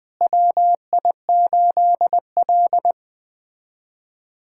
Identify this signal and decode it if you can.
Morse code, heard as a keyed tone, about 20 wpm, 700 Hz. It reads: WI8L
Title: USA Callsign DE Callsign - 4 Characters - 20wpm